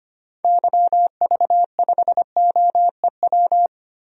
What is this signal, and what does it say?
Morse code, heard as a keyed tone, about 25 wpm, 700 Hz. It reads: YV5OEW
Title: DXCC-100 Call Signs - 6 Characters - Repeated 2x - 25wpm